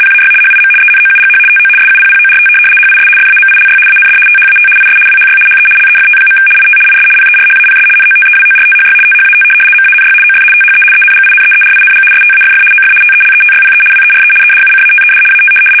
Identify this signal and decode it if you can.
RTTY, about 75 baud. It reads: PAY HIM ENOUGH TO SUPPORT HIS FAMILY AND THAT IS WHY HE LEFT. IT IS PERHAPS NOTEWORTHY THAT MANY OF THE ENGINEERING STAFF OF TELETYPE WERE EDUCATED AT
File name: fsk850-75B / teletype-corp